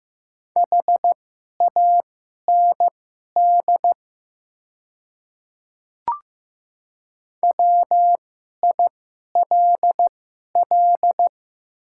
Morse code, heard as a keyed tone, about 15 wpm, spaced as for 11 wpm, 700 Hz. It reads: HAND WILL
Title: 2x Character Spacing - Top 200 Words - 15wpm